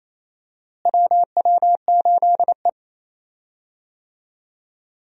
Morse code, heard as a keyed tone, about 27 wpm, 700 Hz. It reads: WW8E
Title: USA Call Signs - 4 Characters - Repeated 3x - 28wpm